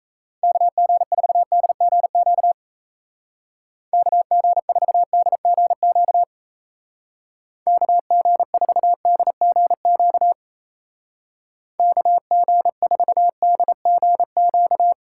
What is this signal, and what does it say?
Morse code, heard as a keyed tone, about 34 wpm, 700 Hz. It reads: KG4DGQ KG4DGQ KG4DGQ KG4DGQ
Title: Speed Racing - USA Call Signs 28wpm